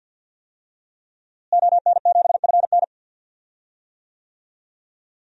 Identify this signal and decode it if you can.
Morse code, heard as a keyed tone, about 53 wpm, 700 Hz. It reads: ON7FN
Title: DXCC-100 Call Signs - 5 Characters - Repeated 2x - 50wpm